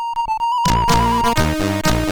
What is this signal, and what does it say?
There is no code.